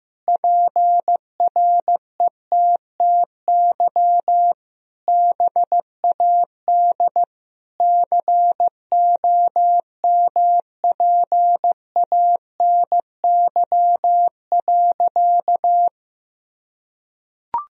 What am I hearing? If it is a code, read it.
Morse code, heard as a keyed tone, about 15 wpm, 700 Hz. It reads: PRETTY BAD COMPANY.
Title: Sentences from Top 1000 Words - Review Part 3 - 15wpm